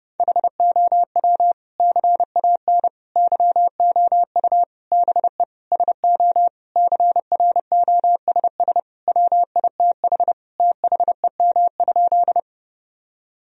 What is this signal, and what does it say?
Morse code, heard as a keyed tone, about 30 wpm, 700 Hz. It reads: HOW CAN YOU BE SO CROSS WITH THEM?
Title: Sentences from Top 1000 Words - Review Part 1 - 30wpm